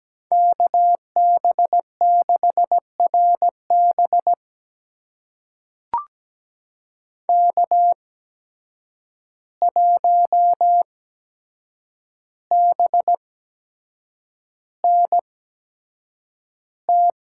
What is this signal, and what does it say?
Morse code, heard as a keyed tone, about 17 wpm, spaced as for 7 wpm, 700 Hz. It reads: KB6RB K1BNT
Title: USA Call Signs - 5 Characters - 8x Character Spacing - 17wpm